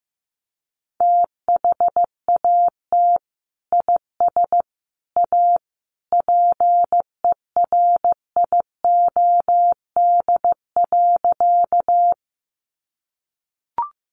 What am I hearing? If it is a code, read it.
Morse code, heard as a keyed tone, about 15 wpm, 700 Hz. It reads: THAT IS A PERIOD.
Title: Sentences from Top 800 Words 15wpm